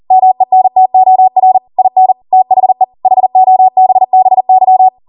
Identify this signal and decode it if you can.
Morse code, heard as a keyed tone, about 40 wpm, 750 Hz. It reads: MENTOR IN THE HOBBY